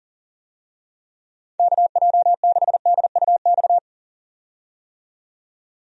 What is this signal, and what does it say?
Morse code, heard as a keyed tone, about 40 wpm, 700 Hz. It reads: KJ6DUX
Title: USA Call Signs - 6 Characters - 40wpm